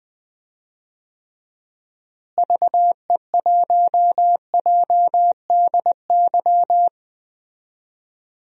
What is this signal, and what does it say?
Morse code, heard as a keyed tone, about 21 wpm, 700 Hz. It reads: VE1JDY
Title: DXCC-100 Call Signs - 6 Characters - Repeated 2x - 20wpm